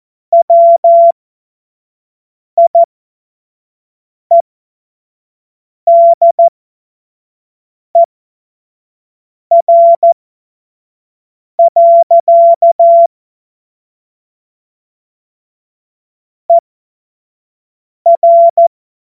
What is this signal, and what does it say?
Morse code, heard as a keyed tone, about 14 wpm, spaced as for 5 wpm, 680 Hz. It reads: WIEDER. ER
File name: Morsekurs / Witze / Witz 28. März 2023 25 F70